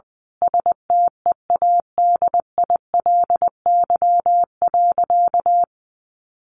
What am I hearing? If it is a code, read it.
Morse code, heard as a keyed tone, about 20 wpm, 700 Hz. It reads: STEADILY.
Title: A Sound Of Thunder 20wpm